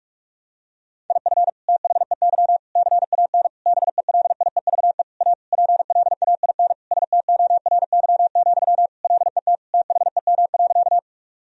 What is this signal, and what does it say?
Morse code, heard as a keyed tone, about 41 wpm, 700 Hz. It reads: IF THEY CAN BELIEVE A PLAIN STORY, LET THEM.
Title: Sentences from Top 1000 Words - Review Part 2 - 45wpm